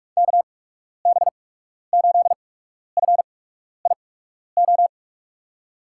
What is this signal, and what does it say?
Morse code, heard as a keyed tone, about 46 wpm, spaced as for 12 wpm, 700 Hz. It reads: KB8FIO
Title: USA Call Signs - 6 Characters - 8x Character Spacing - 45wpm